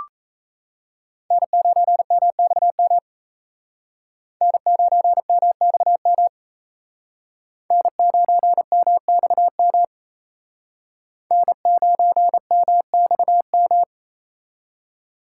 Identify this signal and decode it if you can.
Morse code, heard as a keyed tone, about 37 wpm, 700 Hz. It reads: N9MXM N9MXM N9MXM N9MXM